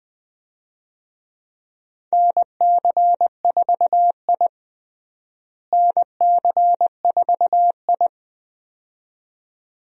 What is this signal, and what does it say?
Morse code, heard as a keyed tone, about 20 wpm, 700 Hz. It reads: NC4I NC4I
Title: USA Call Signs - 4 Characters - Repeated 2x - 20wpm